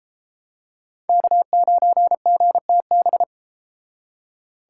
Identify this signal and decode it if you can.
Morse code, heard as a keyed tone, about 31 wpm, 700 Hz. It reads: K9GTB